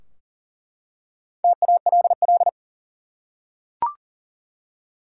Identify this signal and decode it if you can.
Morse code, heard as a keyed tone, about 40 wpm, 700 Hz. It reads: TALL
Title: Four Letter Words - 40wpm